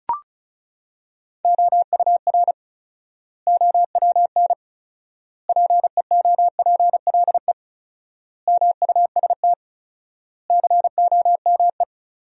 Morse code, heard as a keyed tone, about 36 wpm, 700 Hz. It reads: OUR OWN PEOPLE MUST COME
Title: 4x Word-Spacing - Sentences from Top 300 Words - 35wpm